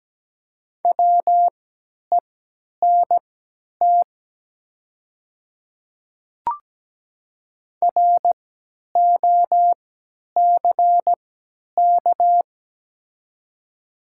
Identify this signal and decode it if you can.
Morse code, heard as a keyed tone, about 17 wpm, spaced as for 10 wpm, 700 Hz. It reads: WENT ROCK
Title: 3x Character Spacing - Top 500 Words - 17wpm